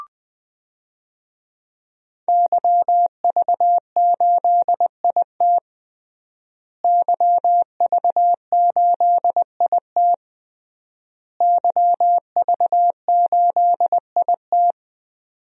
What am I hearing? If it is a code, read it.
Morse code, heard as a keyed tone, about 21 wpm, 700 Hz. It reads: YV8IT YV8IT YV8IT
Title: DXCC-100 Call Signs - 5 Characters - Repeated 3x - 20wpm